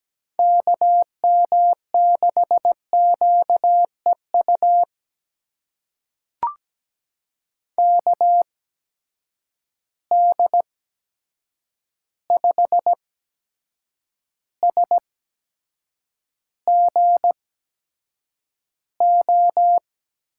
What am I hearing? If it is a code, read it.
Morse code, heard as a keyed tone, about 17 wpm, spaced as for 7 wpm, 700 Hz. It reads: KM6QEU KD5SGO
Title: USA Call Signs - 6 Characters - 8x Character Spacing - 17wpm